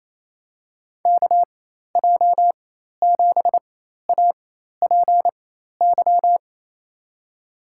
Morse code, heard as a keyed tone, about 28 wpm, spaced as for 13 wpm, 700 Hz. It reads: KJ7APY
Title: USA Call Signs - 6 Characters - 4x Character Spacing - 28wpm